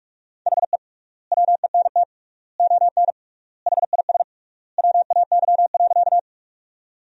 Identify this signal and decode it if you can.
Morse code, heard as a keyed tone, about 42 wpm, 700 Hz. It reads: HE WENT ON HIS WAY.
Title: 3x Word-Spacing - Sentences from Top 200 Words - 45wpm